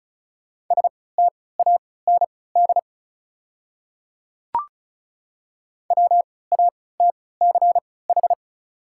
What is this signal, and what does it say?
Morse code, heard as a keyed tone, about 36 wpm, spaced as for 20 wpm, 690 Hz. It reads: STAND WATCH
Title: 3x Character Spacing - Top 300 Words - 35wpm